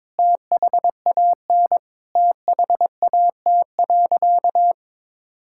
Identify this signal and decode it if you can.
Morse code, heard as a keyed tone, about 22 wpm, 700 Hz. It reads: THAN THAT.
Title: Sentences from Top 1600 Words 22wpm